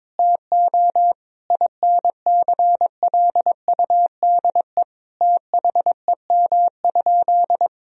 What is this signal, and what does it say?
Morse code, heard as a keyed tone, about 22 wpm, 700 Hz. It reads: TO INCLUDE THEM?
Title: Sentences from Top 1000 Words - Review Part 3 - 22wpm